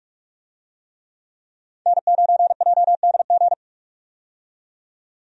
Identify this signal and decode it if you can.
Morse code, heard as a keyed tone, about 46 wpm, 700 Hz. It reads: N9JDG
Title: USA Call Signs - 5 Characters - 45wpm